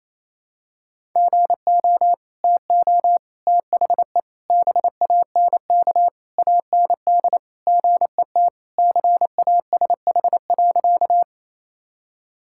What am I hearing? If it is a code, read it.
Morse code, heard as a keyed tone, about 28 wpm, 700 Hz. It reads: GO TO THE BANK AND GET CASH.
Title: Sentences from Top 1000 Words - Review Part 5 - 28wpm